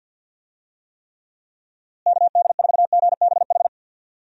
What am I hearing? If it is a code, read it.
Morse code, heard as a keyed tone, about 51 wpm, 700 Hz. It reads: KD4GBH
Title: USA Call Signs - 6 Characters - Repeated 2x - 50wpm